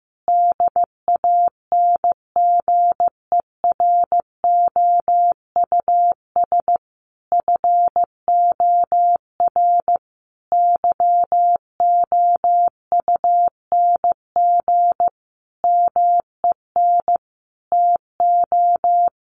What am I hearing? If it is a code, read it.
Morse code, heard as a keyed tone, about 15 wpm, 700 Hz. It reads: DANGEROUS FOR YOUNG MEN TO
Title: Sentences from Top 1200 Words 15wpm